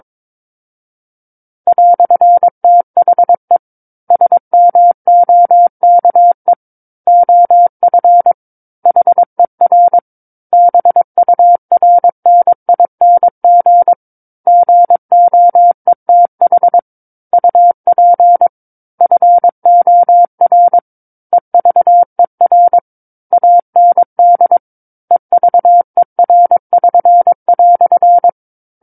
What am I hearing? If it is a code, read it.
Morse code, heard as a keyed tone, about 22 wpm, 700 Hz. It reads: "THE SMOKE OF HER BURNING GOETH UP FOR EVER AND EVER<SN>"
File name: Book One / 13-Chapter 13 - 22 WPM